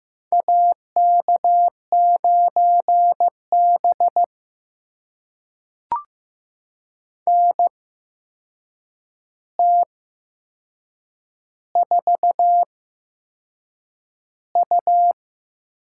Morse code, heard as a keyed tone, about 15 wpm, spaced as for 6 wpm, 700 Hz. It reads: AK9B NT4U
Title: USA Call Signs - 4 Characters - 8x Character Spacing - 15wpm